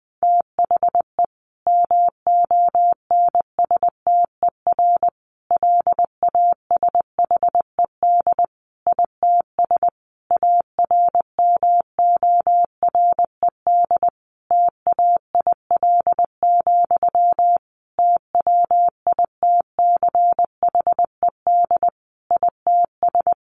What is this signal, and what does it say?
Morse code, heard as a keyed tone, about 20 wpm, 700 Hz. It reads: THE MONSTER LASHED ITS ARMORED TAIL, TWITCHED ITS
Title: A Sound Of Thunder 20wpm